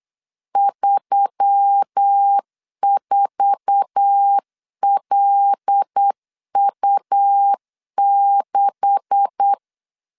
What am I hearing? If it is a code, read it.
Morse code, heard as a keyed tone, about 8 wpm, 800 Hz. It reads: 34LU6